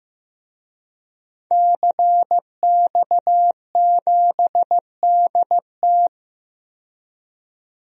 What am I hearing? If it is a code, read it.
Morse code, heard as a keyed tone, about 15 wpm, 700 Hz. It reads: CX7DT
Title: DXCC-100 Call Signs - 5 Characters - 15wpm